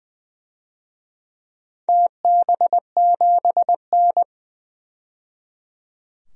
Morse code, heard as a keyed tone, about 20 wpm, 700 Hz. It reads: TB7N
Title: DXCC-100 Call Signs - 4 Characters - 20wpm